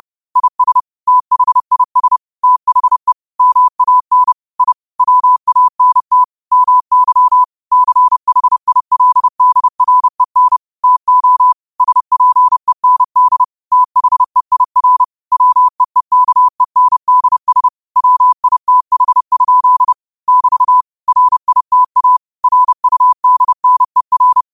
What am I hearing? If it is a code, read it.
Morse code, heard as a keyed tone, about 30 wpm, 1000 Hz. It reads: IS THIS THE MAN I WANT MY CHILDREN TO SPEND THEIR WEEKENDS WITH? <BT> RITA RUDNER
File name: qotdfiles / 30wpm